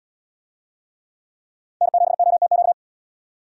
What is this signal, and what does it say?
Morse code, heard as a keyed tone, about 77 wpm, 700 Hz. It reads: N7OTQ